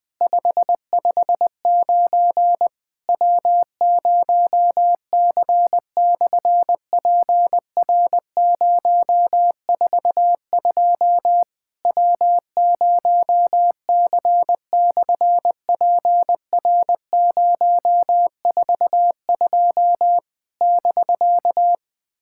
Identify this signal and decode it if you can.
Morse code, heard as a keyed tone, about 20 wpm, 700 Hz. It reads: 559 W0C/PR042 W0C/PR042 <BK>
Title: SOTA QSO Practice 20wpm